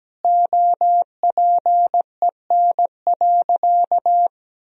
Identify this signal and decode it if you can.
Morse code, heard as a keyed tone, about 17 wpm, 700 Hz. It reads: OPEN.
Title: Sentences from Top 1000 Words - Review Part 3 - 17wpm